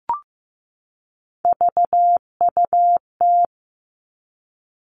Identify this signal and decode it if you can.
Morse code, heard as a keyed tone, about 15 wpm, 700 Hz. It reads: VUT
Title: Three Challenging Letter-Number Combinations 15wpm